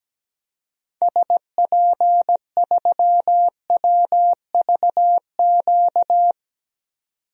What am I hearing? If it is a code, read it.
Morse code, heard as a keyed tone, about 17 wpm, 700 Hz. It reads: SP3WVQ
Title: DXCC-100 Call Signs - 6 Characters - 17wpm